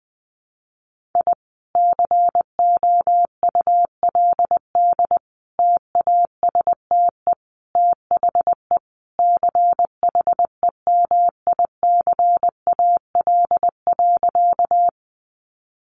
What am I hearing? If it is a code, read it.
Morse code, heard as a keyed tone, about 20 wpm, 700 Hz. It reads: I COULD TASTE THE CHEMICAL.
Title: Sentences from Top 1200 Words 20wpm